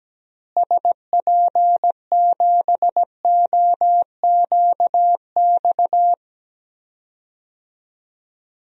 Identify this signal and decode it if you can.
Morse code, heard as a keyed tone, about 17 wpm, 700 Hz. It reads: SP7OQX
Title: DXCC-100 Call Signs - 6 Characters - Repeated 2x - 17wpm